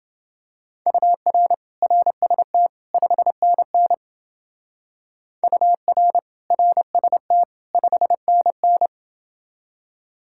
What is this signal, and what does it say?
Morse code, heard as a keyed tone, about 29 wpm, 700 Hz. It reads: UR RST 5NN UR RST 5NN